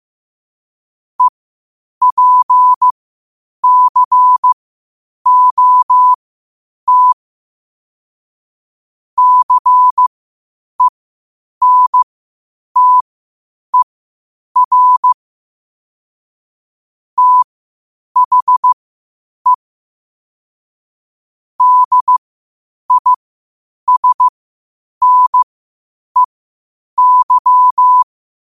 Morse code, heard as a keyed tone, about 15 wpm, spaced as for 8 wpm, 1000 Hz. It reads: EPCOT CENTER THE DISNEY
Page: Quotes for Fri, 15 Aug 2025 in Morse Code at 8 words per minute.